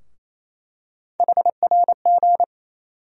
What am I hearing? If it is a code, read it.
Morse code, heard as a keyed tone, about 28 wpm, 700 Hz. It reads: HRG
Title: USA Call Sign Postfixes - 3 Letters 28wpm